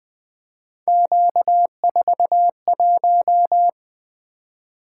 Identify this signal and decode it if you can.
Morse code, heard as a keyed tone, about 20 wpm, 700 Hz. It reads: Q41
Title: Three Letter-Number Combinations - 20wpm